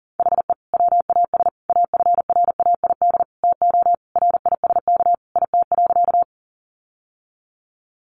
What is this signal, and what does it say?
Morse code, heard as a keyed tone, about 40 wpm, 700 Hz. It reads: HE WAS AFRAID TO RISK IT.
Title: Sentences from Top 1500 Words 40wpm